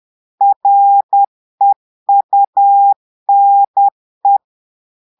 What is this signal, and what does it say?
Morse code, heard as a keyed tone, about 10 wpm, 800 Hz. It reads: REUNE
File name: Portugais - texte en clair 1 - vitesse - 10 mots minute